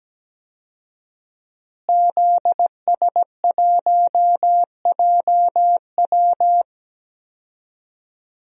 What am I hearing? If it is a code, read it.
Morse code, heard as a keyed tone, about 17 wpm, 700 Hz. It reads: ZS1JW
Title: DXCC-100 Call Signs - 5 Characters - Repeated 3x - 17wpm